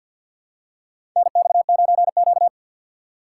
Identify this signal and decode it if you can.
Morse code, heard as a keyed tone, about 45 wpm, 700 Hz. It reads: NX9Q
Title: USA Call Signs - 4 Characters - 2x Character Spacing - 50wpm